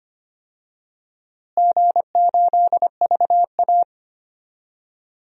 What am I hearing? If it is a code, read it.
Morse code, heard as a keyed tone, about 25 wpm, 700 Hz. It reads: G8VA
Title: DXCC-100 Call Signs - 4 Characters - Repeated 3x - 25wpm